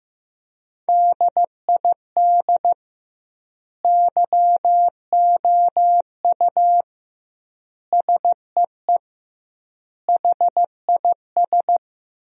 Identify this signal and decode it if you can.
Morse code, heard as a keyed tone, about 15 wpm, 700 Hz. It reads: DID YOU SEE HIS
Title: 2x Word-Spacing - Sentences from Top 300 Words - 15wpm